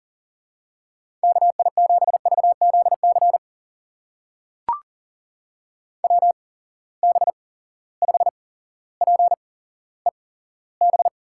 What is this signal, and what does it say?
Morse code, heard as a keyed tone, about 40 wpm, spaced as for 17 wpm, 700 Hz. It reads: KI7VZC WB5PEB
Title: USA Call Signs - 6 Characters - 8x Character Spacing - 40wpm